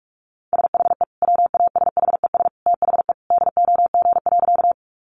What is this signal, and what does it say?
Morse code, heard as a keyed tone, about 44 wpm, 700 Hz. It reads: SHE WASHES THE DOG.
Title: Easy Phrases 45wpm